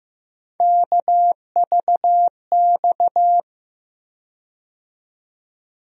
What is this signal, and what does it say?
Morse code, heard as a keyed tone, about 15 wpm, 700 Hz. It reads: KVX
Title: USA Call Sign Postfixes - 3 Letters 15wpm